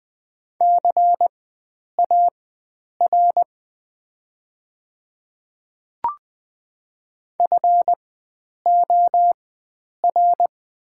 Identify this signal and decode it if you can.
Morse code, heard as a keyed tone, about 20 wpm, spaced as for 9 wpm, 700 Hz. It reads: CAR FOR
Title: 4x Character Spacing - Top 300 Words - 20wpm